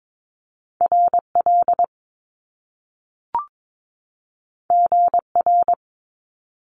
Morse code, heard as a keyed tone, about 22 wpm, 700 Hz. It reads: RL GR
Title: Top Two Letter Combinations in Words 22wpm